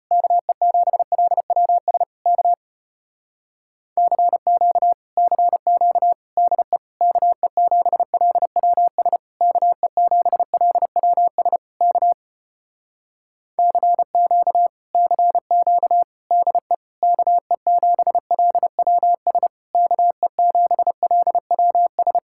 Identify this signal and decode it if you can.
Morse code, heard as a keyed tone, about 33 wpm, 700 Hz. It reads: KE7LWS K CQ CQ DE KE7LWS KE7LWS K CQ CQ DE KE7LWS KE7LWS